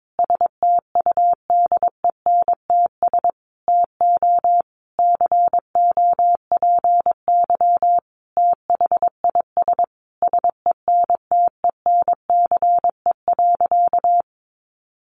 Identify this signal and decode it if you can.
Morse code, heard as a keyed tone, about 22 wpm, 700 Hz. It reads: STUDENTS TO COPY THIS SENTENCE.
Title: Additional Easy Phrases 22wpm